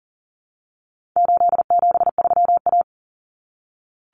Morse code, heard as a keyed tone, about 40 wpm, 700 Hz. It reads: 873A
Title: Common Cut Numbers - 4-Digit 40wpm